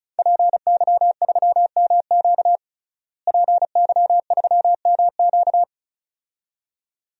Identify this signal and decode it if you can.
Morse code, heard as a keyed tone, about 36 wpm, 700 Hz. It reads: PY3MQ PY3MQ
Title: DXCC-100 Call Signs - 5 Characters - Repeated 2x - 35wpm